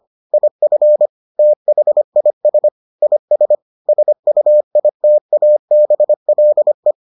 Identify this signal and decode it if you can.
Morse code, heard as a keyed tone, about 25 wpm, 600 Hz. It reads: IF THIS IS SUITABLE